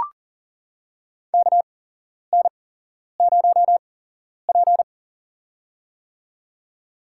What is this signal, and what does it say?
Morse code, heard as a keyed tone, about 40 wpm, spaced as for 11 wpm, 700 Hz. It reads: KN0P
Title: USA Call Signs - 4 Characters - 8x Character Spacing - 40wpm